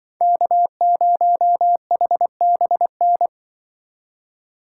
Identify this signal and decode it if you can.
Morse code, heard as a keyed tone, about 24 wpm, 700 Hz. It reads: K0HBN
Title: Speed Racing - USA 1x3 Call Signs 20wpm